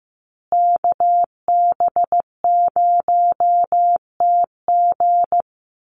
Morse code, heard as a keyed tone, about 15 wpm, 700 Hz. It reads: KB0TG
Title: USA Call Signs Repeated 3x 15wpm